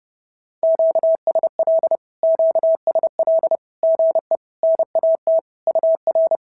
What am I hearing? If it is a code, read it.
Morse code, heard as a keyed tone, about 30 wpm, 650 Hz. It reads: QSL QSL GE NAT UR
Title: SOTA QSO Practice 30wpm